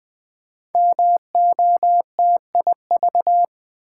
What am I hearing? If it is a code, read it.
Morse code, heard as a keyed tone, about 20 wpm, 700 Hz. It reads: MOTIV
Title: Build A Word - 5 Letters At A Time - 10-Letter Words - 20wpm